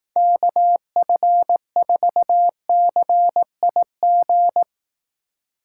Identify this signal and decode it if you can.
Morse code, heard as a keyed tone, about 18 wpm, 700 Hz. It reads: KF4CIG
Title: Speed Racing - USA 2x3 Call Signs 15wpm